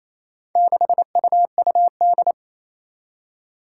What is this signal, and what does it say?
Morse code, heard as a keyed tone, about 28 wpm, 700 Hz. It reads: 6UUD
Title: Four Challenging Letter-Number Combinations 28wpm